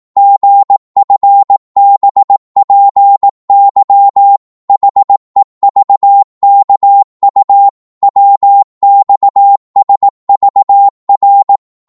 Morse code, heard as a keyed tone, about 18 wpm, 800 Hz. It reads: GFBPY HEVKU WXSVR
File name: Morse / ltr-18wpm